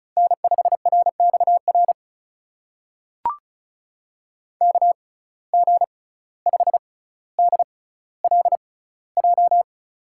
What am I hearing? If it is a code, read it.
Morse code, heard as a keyed tone, about 36 wpm, spaced as for 17 wpm, 700 Hz. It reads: N5RXR KG5DLJ
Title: USA Call Signs - 6x Character Spacing - 35wpm